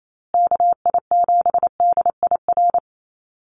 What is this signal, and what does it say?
Morse code, heard as a keyed tone, about 28 wpm, 700 Hz. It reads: KI7DIR
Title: USA Call Signs Repeated 3x 28wpm